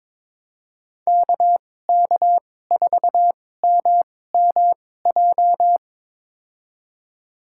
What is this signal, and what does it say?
Morse code, heard as a keyed tone, about 22 wpm, spaced as for 16 wpm, 700 Hz. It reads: KK4MMJ
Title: USA Call Signs - 6 Characters - 2x Character Spacing - 22wpm